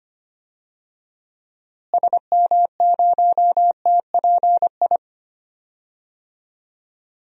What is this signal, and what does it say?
Morse code, heard as a keyed tone, about 25 wpm, 700 Hz. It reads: SM0TPI
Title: DXCC-100 Call Signs - 6 Characters - Repeated 2x - 25wpm